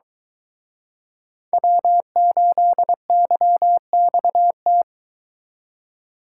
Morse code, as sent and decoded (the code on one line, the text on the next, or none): .-- ---.. -.-- -..- -
W8YXT